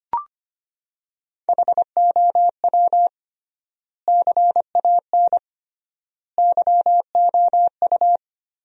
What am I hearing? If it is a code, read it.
Morse code, heard as a keyed tone, about 25 wpm, 700 Hz. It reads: HOW CAN YOU
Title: 3x Word-Spacing - Sentences from Top 200 Words - 25wpm